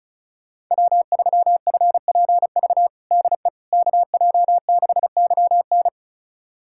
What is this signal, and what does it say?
Morse code, heard as a keyed tone, about 36 wpm, 700 Hz. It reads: W3FPV DE KJ6YN
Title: USA Callsign DE Callsign - 5 Characters - 35wpm